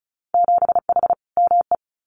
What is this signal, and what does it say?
Morse code, heard as a keyed tone, about 36 wpm, 700 Hz. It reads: 7H ME
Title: Winter Field Day Exchanges Encore 35wpm